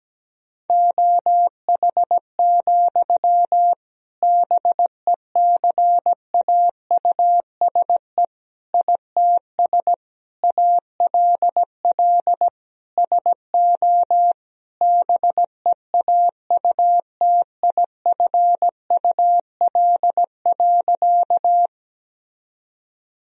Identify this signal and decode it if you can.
Morse code, heard as a keyed tone, about 17 wpm, 700 Hz. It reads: OH, BECAUSE ITS ALL SO BEAUTIFUL.
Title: Sentences from Top 1000 Words - Review Part 3 - 17wpm